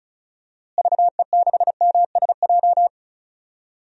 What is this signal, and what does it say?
Morse code, heard as a keyed tone, about 36 wpm, 700 Hz. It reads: VE6MSJ
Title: DXCC-100 Call Signs - 6 Characters - Repeated 3x - 35wpm